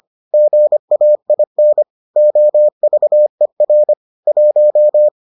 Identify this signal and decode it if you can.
Morse code, heard as a keyed tone, about 25 wpm, 600 Hz. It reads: GAIN OVER 1